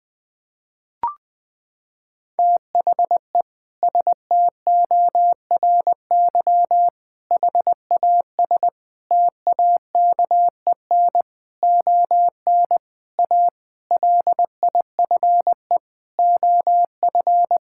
Morse code, heard as a keyed tone, about 20 wpm, 700 Hz. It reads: THE STORY HAS TAKEN ON A LIFE OF
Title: Sentences with 8-Word Phrases - 20wpm